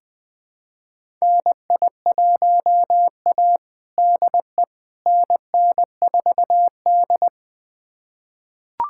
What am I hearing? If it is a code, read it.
Morse code, heard as a keyed tone, about 21 wpm, 700 Hz. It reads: NI1A DE NN4D
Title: USA Callsign DE Callsign - 4 Characters - 20wpm